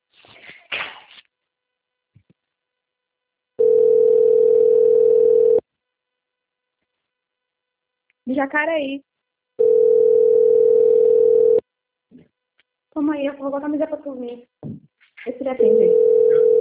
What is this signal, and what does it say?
Ringback tone (North America), 440 + 480 Hz, with speech.
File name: out- - - - - .26325